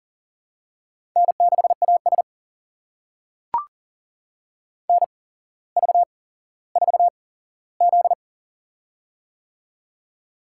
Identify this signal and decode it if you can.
Morse code, heard as a keyed tone, about 38 wpm, spaced as for 17 wpm, 700 Hz. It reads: N6AS NV4Z
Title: USA Call Signs - 4 Characters - 8x Character Spacing - 40wpm